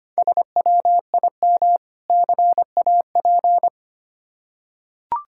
Morse code, heard as a keyed tone, about 25 wpm, 700 Hz. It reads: SWIM CAP
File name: practice / 25wpm / Difficult Words with Hints - 25wpm